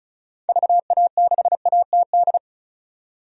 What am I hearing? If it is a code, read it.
Morse code, heard as a keyed tone, about 34 wpm, 700 Hz. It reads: VA6ATD
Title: DXCC-100 Call Signs - 6 Characters - 35wpm